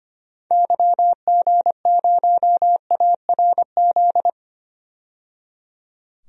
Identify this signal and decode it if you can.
Morse code, heard as a keyed tone, about 25 wpm, 690 Hz. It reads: YG0ARZ
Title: DXCC-100 Call Signs - 6 Characters - 25wpm